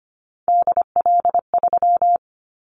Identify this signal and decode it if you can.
Morse code, heard as a keyed tone, about 25 wpm, 700 Hz. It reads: DL3
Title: Callsign Prefixes for Top 50 DXCC - 25wpm